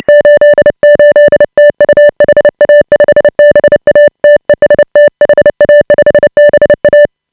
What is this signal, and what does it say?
Morse code, heard as a keyed tone, about 30 wpm, 590 Hz. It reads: 88 TU HA5BA TEST HA5BA
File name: AUDIO2018 / HA5BA CW